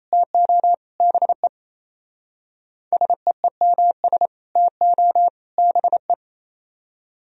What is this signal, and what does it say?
Morse code, heard as a keyed tone, about 33 wpm, 700 Hz. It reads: TO BE SEEMS TO BE
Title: Speed-Racing - Sets of 3 Words 28wpm